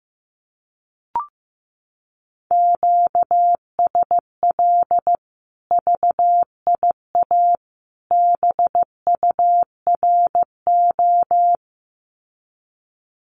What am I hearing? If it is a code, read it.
Morse code, heard as a keyed tone, about 15 wpm, 700 Hz. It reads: QSL VIA BURO
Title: Common QSO Phrases 15wpm